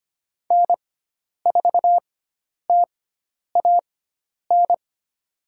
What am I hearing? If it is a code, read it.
Morse code, heard as a keyed tone, about 25 wpm, spaced as for 10 wpm, 700 Hz. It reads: N4TAN
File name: Lesson 16g - Letter c - Call Signs 25wpm